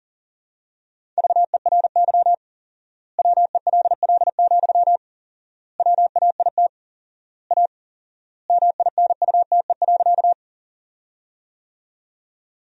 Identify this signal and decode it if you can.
Morse code, heard as a keyed tone, about 40 wpm, 700 Hz. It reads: VERY WELL, WAIT A MINUTE.
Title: 4x Word-Spacing - Sentences from Top 500 Words - 40wpm